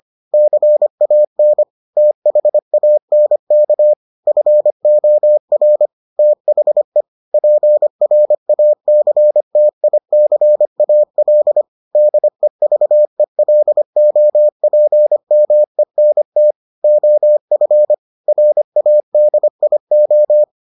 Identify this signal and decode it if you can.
Morse code, heard as a keyed tone, about 25 wpm, 600 Hz. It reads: CAN THANK FOR THE PRACTICAL DEVELOPMENT OF RADIO